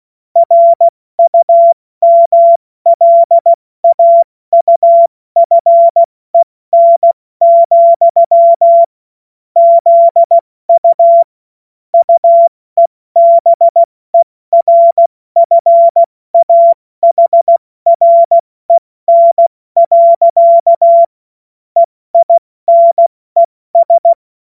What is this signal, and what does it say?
Morse code, heard as a keyed tone, about 16 wpm, spaced as for 14 wpm, 680 Hz. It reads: RUMLAUFEN, ZU UEBERFAHREN. EINES
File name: Morsekurs / Witze / Witz 31. Mai 2022 70 F80